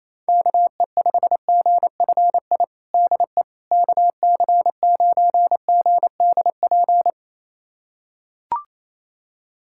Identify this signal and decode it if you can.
Morse code, heard as a keyed tone, about 27 wpm, 700 Hz. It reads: KE5GFI DE KC9GDP
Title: USA Callsign DE Callsign 28wpm